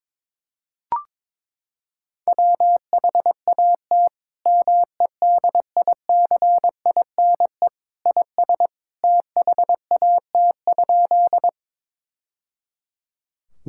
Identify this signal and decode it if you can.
Morse code, heard as a keyed tone, about 22 wpm, 700 Hz. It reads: WHAT MEDICINE IS THAT?
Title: Sentences from Top 1000 Words - Review Part 6 - 22wpm